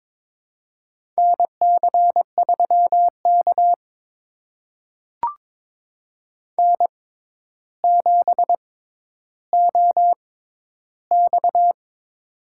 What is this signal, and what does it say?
Morse code, heard as a keyed tone, about 22 wpm, spaced as for 11 wpm, 700 Hz. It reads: NC3K N7OX
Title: USA Call Signs - 4 Characters - 6x Character Spacing - 22wpm